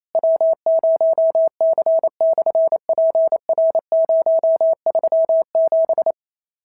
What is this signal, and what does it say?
Morse code, heard as a keyed tone, about 28 wpm, 650 Hz. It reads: W0C/PR037
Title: SOTA QSO Practice 28wpm